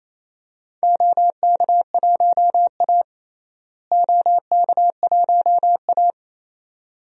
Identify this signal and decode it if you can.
Morse code, heard as a keyed tone, about 28 wpm, 700 Hz. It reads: OK1A OK1A
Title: DXCC-100 Call Signs - 4 Characters - Repeated 2x - 28wpm